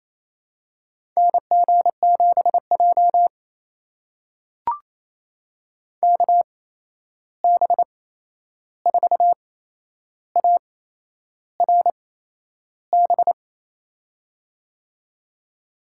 Morse code, heard as a keyed tone, about 28 wpm, spaced as for 11 wpm, 700 Hz. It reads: NG7J KB4ARB
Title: USA Call Signs - 8x Character Spacing - 28wpm